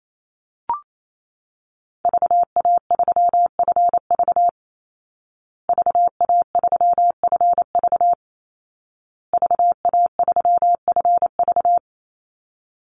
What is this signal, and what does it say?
Morse code, heard as a keyed tone, about 28 wpm, 700 Hz. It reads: VA3FV VA3FV VA3FV
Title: Call Signs Repeated 2x - 28wpm